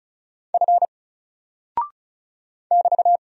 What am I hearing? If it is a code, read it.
Morse code, heard as a keyed tone, about 36 wpm, 700 Hz. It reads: F <BT>
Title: Single Character (Letter:Number:Punctuation:Prosign) 35wpm